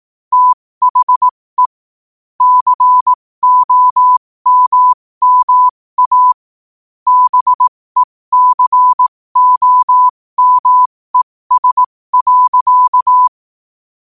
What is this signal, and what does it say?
Morse code, heard as a keyed tone, about 18 wpm, spaced as for 15 wpm, 1000 Hz. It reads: THE COMMA BECOMES.
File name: qotdfiles / 15wpm